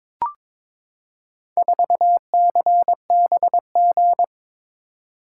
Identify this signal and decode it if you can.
Morse code, heard as a keyed tone, about 22 wpm, 700 Hz. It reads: 4CBG
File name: practice / 22wpm / Four Letter-Number Combinations - 22wpm